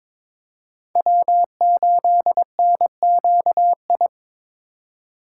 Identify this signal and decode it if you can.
Morse code, heard as a keyed tone, about 26 wpm, 700 Hz. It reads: W8NQI
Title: Speed Racing - USA 1x3 Call Signs 22wpm